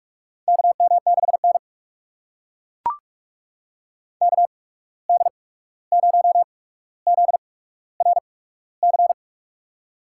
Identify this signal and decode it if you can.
Morse code, heard as a keyed tone, about 43 wpm, 690 Hz. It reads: K M 6 N K D 0 Z R C
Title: USA Call Signs - 8x Character Spacing - 45wpm